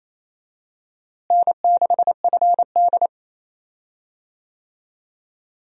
Morse code, heard as a keyed tone, about 28 wpm, 700 Hz. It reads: N6FD
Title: USA Call Signs - 4 Characters - Repeated 2x - 28wpm